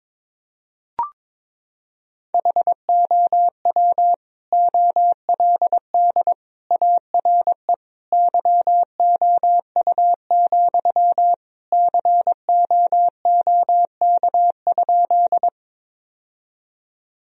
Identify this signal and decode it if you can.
Morse code, heard as a keyed tone, about 21 wpm, 700 Hz. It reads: HOW OLD ARE YOU, COOK?
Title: Sentences from Top 1000 Words - Review Part 4 - 22wpm